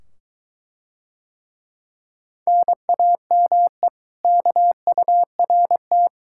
Morse code, heard as a keyed tone, about 23 wpm, 700 Hz. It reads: NAME KURT